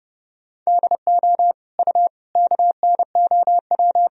Morse code, heard as a keyed tone, about 30 wpm, 700 Hz. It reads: DO U KNOW
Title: Sentences from Top 100 Words and Abbreviations 30wpm